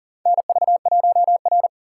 Morse code, heard as a keyed tone, about 37 wpm, 700 Hz. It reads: NV1R
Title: USA Call Signs - 4 Characters - 3x Character Spacing - 40wpm